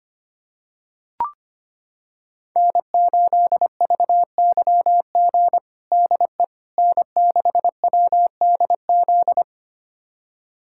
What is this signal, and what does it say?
Morse code, heard as a keyed tone, about 25 wpm, 700 Hz. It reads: N8VYG DE N6WDZ
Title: USA Callsign DE Callsign - 5 Characters - 25wpm